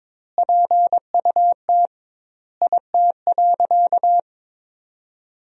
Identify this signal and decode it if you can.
Morse code, heard as a keyed tone, about 21 wpm, 700 Hz. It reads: PUT IT.
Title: 2x Word-Spacing - Sentences from Top 200 Words - 22wpm